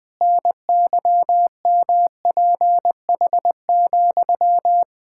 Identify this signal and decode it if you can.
Morse code, heard as a keyed tone, about 19 wpm, 700 Hz. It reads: NYMPH,
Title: Pangram Challenge v2 - 20wpm